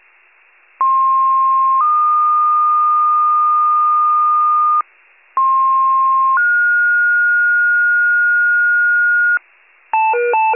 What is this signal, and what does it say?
Two-tone page, heard as a one-second tone, then a three-second tone, about 1060 then 1220 Hz, about 1060 then 1430 Hz, with noise. It is heard from 0.8 to 9.4 s.